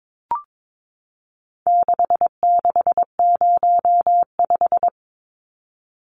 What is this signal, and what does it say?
Morse code, heard as a keyed tone, about 22 wpm, 700 Hz. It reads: 6605
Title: Numbers - 4-Digit 22wpm